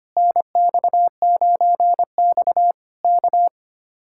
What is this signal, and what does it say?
Morse code, heard as a keyed tone, about 24 wpm, 700 Hz. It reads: NX9X K
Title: Speed-Racing Common QSO Phrases Encore - 25wpm